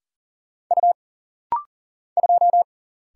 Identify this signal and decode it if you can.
Morse code, heard as a keyed tone, about 40 wpm, 700 Hz. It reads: U 2
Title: Single Letter-Number Alt - Rapid-Fire 40wpm